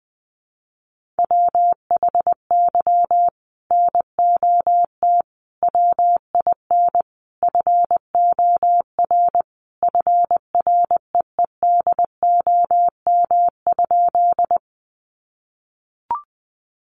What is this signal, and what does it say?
Morse code, heard as a keyed tone, about 20 wpm, 700 Hz. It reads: WHY NOT WIN FOR FREEDOM?
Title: Sentences from Top 1500 Words 20wpm